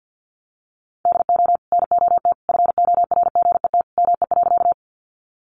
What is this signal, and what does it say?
Morse code, heard as a keyed tone, about 50 wpm, 700 Hz. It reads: DO NOT FORGET ME.
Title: Sentences from Top 1200 Words 50wpm